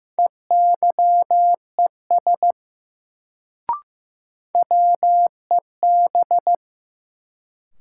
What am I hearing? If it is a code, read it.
Morse code, heard as a keyed tone, about 16 wpm, 700 Hz. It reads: EYES WEB
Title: Sentences from Top 1000 Words - Review Part 4 - New Word Drill - 15wpm